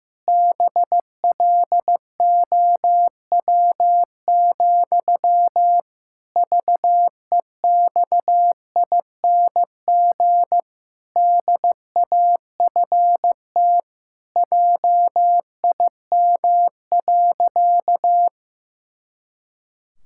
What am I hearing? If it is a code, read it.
Morse code, heard as a keyed tone, about 15 wpm, 700 Hz. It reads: BLOW, VEXING DAFT JIM.
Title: Pangram Challenge v2 - 15wpm